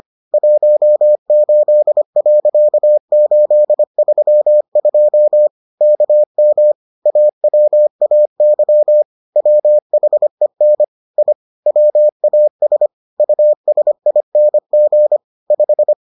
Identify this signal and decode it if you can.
Morse code, heard as a keyed tone, about 25 wpm, 600 Hz. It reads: 18.832 KM AWAY WHEN I WAS USING 5